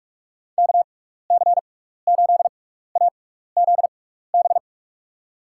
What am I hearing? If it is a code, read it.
Morse code, heard as a keyed tone, about 41 wpm, spaced as for 15 wpm, 700 Hz. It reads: KC8AZB
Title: USA Call Signs - 6 Characters - 6x Character Spacing - 45wpm